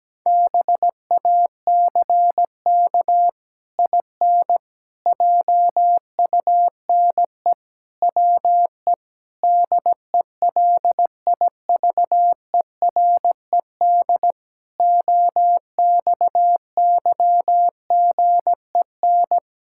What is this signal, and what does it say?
Morse code, heard as a keyed tone, about 17 wpm, 700 Hz. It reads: BACK IN JUNE WE DELIVERED OXYGEN
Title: Pangram Challenge v2 - 17wpm